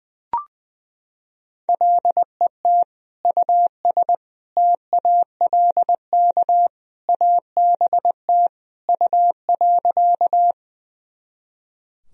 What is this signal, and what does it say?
Morse code, heard as a keyed tone, about 20 wpm, 700 Hz. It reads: LET US TALK ABT U.
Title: Sentences from Top 400 Words and Abbreviations 20wpm